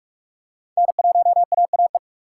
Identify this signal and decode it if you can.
Morse code, heard as a keyed tone, about 40 wpm, 700 Hz. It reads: N1AAE